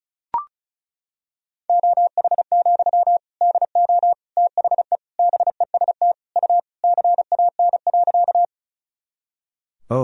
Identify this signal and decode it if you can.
Morse code, heard as a keyed tone, about 36 wpm, 700 Hz. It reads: OH, DO THE BEST U CAN.
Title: Sentences from Top 500 Words and Abbreviations 35wpm